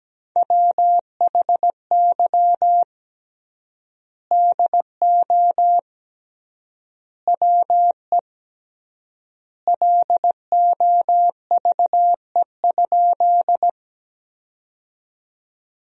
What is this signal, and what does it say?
Morse code, heard as a keyed tone, about 17 wpm, 700 Hz. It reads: WHY DO WE LOVE?
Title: 3x Word-Spacing - Sentences from Top 400 Words - 17wpm